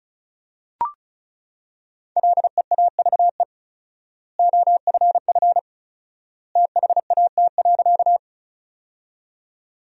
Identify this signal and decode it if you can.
Morse code, heard as a keyed tone, about 34 wpm, 700 Hz. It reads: LEAVE OFF THAT.
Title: 4x Word-Spacing - Sentences from Top 400 Words - 35wpm